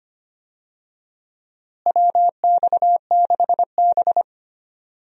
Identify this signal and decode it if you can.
Morse code, heard as a keyed tone, about 25 wpm, 700 Hz. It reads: WX6B
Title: USA Call Signs - 4 Characters - Repeated 2x - 25wpm